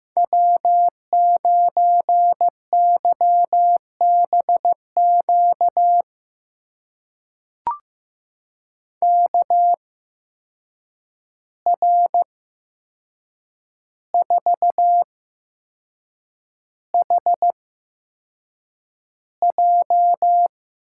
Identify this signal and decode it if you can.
Morse code, heard as a keyed tone, about 15 wpm, spaced as for 6 wpm, 700 Hz. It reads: W9YBQ KR4HJ
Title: USA Call Signs - 5 Characters - 8x Character Spacing - 15wpm